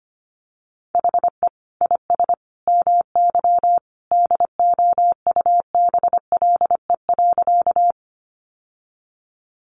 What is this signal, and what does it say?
Morse code, heard as a keyed tone, about 25 wpm, 700 Hz. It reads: HE IS MY DOUBLE.
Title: Sentences from Top 1400 Words 25wpm